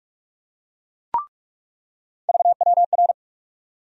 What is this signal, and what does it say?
Morse code, heard as a keyed tone, about 45 wpm, 700 Hz. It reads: VWR
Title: USA Call Sign Postfixes - 3 Letters 45wpm